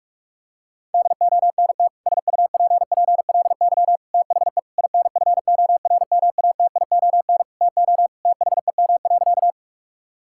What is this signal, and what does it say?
Morse code, heard as a keyed tone, about 42 wpm, 700 Hz. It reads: DONT SUPPLY THE INFORMATION TO THEM.
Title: Sentences from Top 1000 Words - Review Part 5 - 45wpm